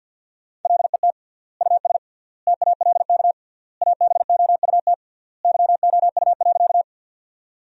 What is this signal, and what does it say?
Morse code, heard as a keyed tone, about 51 wpm, 700 Hz. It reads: LET US TALK ABOUT YOU.
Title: 3x Word-Spacing - Sentences from Top 400 Words - 50wpm